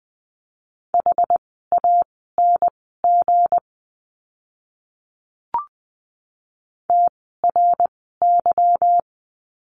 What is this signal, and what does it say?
Morse code, heard as a keyed tone, about 20 wpm, spaced as for 14 wpm, 700 Hz. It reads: HANG TRY
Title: 160 Lesson 26f - Letter g - Words 20wpm